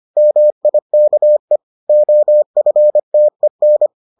Morse code, heard as a keyed tone, about 25 wpm, 600 Hz. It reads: MIKE OFTEN